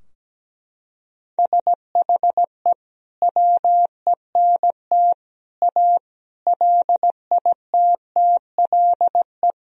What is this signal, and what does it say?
Morse code, heard as a keyed tone, about 17 wpm, 700 Hz. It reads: SHE WENT A LITTLE
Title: Sentences from Top 1000 Words - Review Part 2 - 17wpm